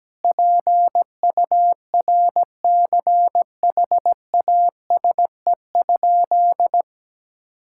Morse code, heard as a keyed tone, about 17 wpm, 700 Hz. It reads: PURCHASE?
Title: Sentences from Top 1000 Words - Review Part 6 - 17wpm